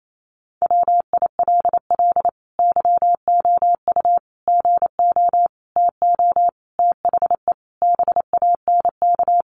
Morse code, heard as a keyed tone, about 28 wpm, 690 Hz. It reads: WILL YOU GO TO THE BANK
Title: Sentences from Top 900 Words 28wpm